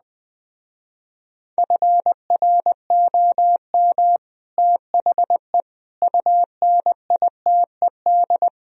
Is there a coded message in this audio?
Yes: Morse code from 1.6 to 8.5 s, heard as a keyed tone, about 20 wpm, 700 Hz. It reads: FROM THE UNITED